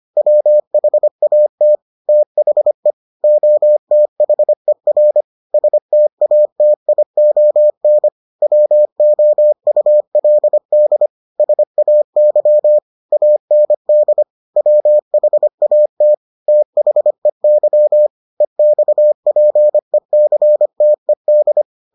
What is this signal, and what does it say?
Morse code, heard as a keyed tone, about 25 wpm, 600 Hz. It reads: WHAT THE OTHER STATION WOULD SAY AND WHAT THEY EXPECTED